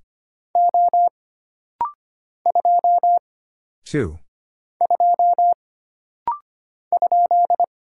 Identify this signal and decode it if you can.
Morse code, heard as a keyed tone, about 25 wpm, 700 Hz. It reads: O 2 2 ?
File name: Single Character (Letter:Number:Punctuation:Prosign) - Rapid-Fire - 25wpm